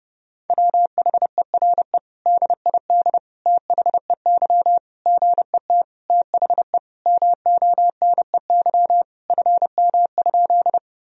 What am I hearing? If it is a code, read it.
Morse code, heard as a keyed tone, about 30 wpm, 700 Hz. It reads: WHERE DID THEY GET THE MONEY FM?
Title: Sentences from Top 400 Words and Abbreviations 30wpm